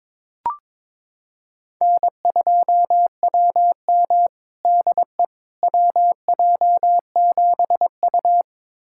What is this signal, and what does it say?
Morse code, heard as a keyed tone, about 21 wpm, 700 Hz. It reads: N2WM DE WJ7U
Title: USA Callsign DE Callsign - 4 Characters - 22wpm